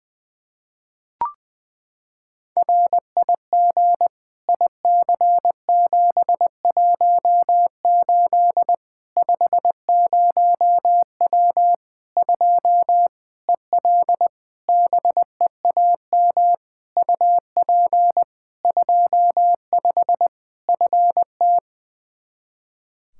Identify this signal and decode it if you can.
Morse code, heard as a keyed tone, about 20 wpm, 700 Hz. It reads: RIG IC718 50W 2 EL BEAM UP 25 FT
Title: K1USN SST QSO Practice 20wpm